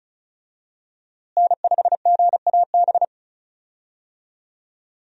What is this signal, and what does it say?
Morse code, heard as a keyed tone, about 34 wpm, 700 Hz. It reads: N5GAB
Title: USA Call Signs - 5 Characters - Repeated 2x - 35wpm